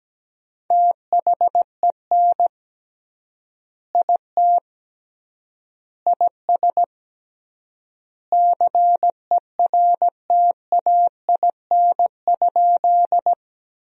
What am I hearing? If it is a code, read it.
Morse code, heard as a keyed tone, about 17 wpm, 700 Hz. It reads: THEN IT IS CERTAIN?
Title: 3x Word-Spacing - Sentences from Top 500 Words - 17wpm